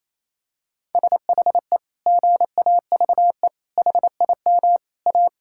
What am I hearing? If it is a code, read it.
Morse code, heard as a keyed tone, about 28 wpm, 700 Hz. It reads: SHE GAVE HIM A
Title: Sentences from Top 1600 Words 28wpm